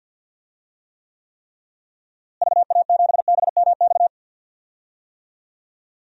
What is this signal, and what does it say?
Morse code, heard as a keyed tone, about 52 wpm, 700 Hz. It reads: VA7BMX
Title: DXCC-100 Call Signs - 6 Characters - Repeated 3x - 50wpm